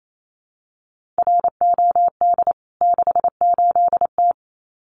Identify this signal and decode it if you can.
Morse code, heard as a keyed tone, about 28 wpm, 700 Hz. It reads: ROD 68T
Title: CWT Exchange 28wpm